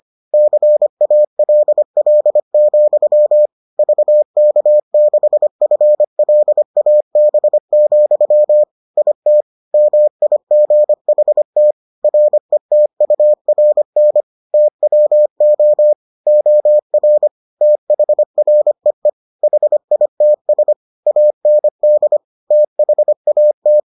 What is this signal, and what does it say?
Morse code, heard as a keyed tone, about 25 wpm, 600 Hz. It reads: CALL, VK6FLAB, IT MIGHT RETURN TWO OR THREE HITS AND THAT